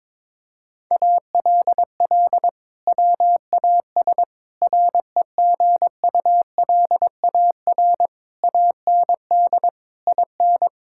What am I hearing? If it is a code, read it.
Morse code, heard as a keyed tone, about 22 wpm, 700 Hz. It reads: ALL WAS REGULAR AND IN
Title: Sentences from Top 1000 Words - Review Part 6 - 22wpm